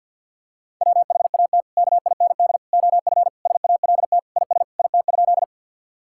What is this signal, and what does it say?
Morse code, heard as a keyed tone, about 50 wpm, 700 Hz. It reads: WHAT KIND OF SALT IS IT?
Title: Sentences from Top 1000 Words - Review Part 5 - 50wpm